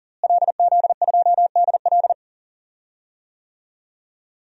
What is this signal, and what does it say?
Morse code, heard as a keyed tone, about 40 wpm, 700 Hz. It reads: LZ2DL
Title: DXCC-100 Call Signs - 5 Characters - Repeated 3x - 40wpm